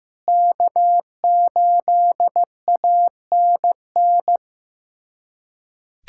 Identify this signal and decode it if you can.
Morse code, heard as a keyed tone, about 16 wpm, 700 Hz. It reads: K8ANN
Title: USA Callsign DE Callsign - 5 Characters - 15wpm